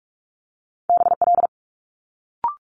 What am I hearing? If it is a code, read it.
Morse code, heard as a keyed tone, about 46 wpm, 700 Hz. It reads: BL
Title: Multi-letter Phonemes Quick Fire 45wpm